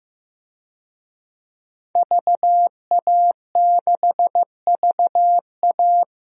Morse code, heard as a keyed tone, about 16 wpm, 700 Hz. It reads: VA6VA
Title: DXCC-100 Call Signs - 5 Characters - Repeated 3x - 15wpm